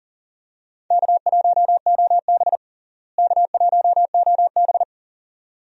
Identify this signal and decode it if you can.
Morse code, heard as a keyed tone, about 40 wpm, 700 Hz. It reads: K1OB K1OB
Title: USA Call Signs - 4 Characters - Repeated 2x - 40wpm